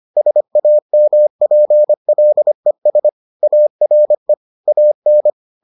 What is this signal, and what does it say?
Morse code, heard as a keyed tone, about 25 wpm, 600 Hz. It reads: SAMPLES ARE AN